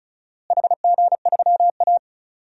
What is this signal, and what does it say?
Morse code, heard as a keyed tone, about 34 wpm, 700 Hz. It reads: HG3A
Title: DXCC-100 Call Signs - 4 Characters - Repeated 3x - 35wpm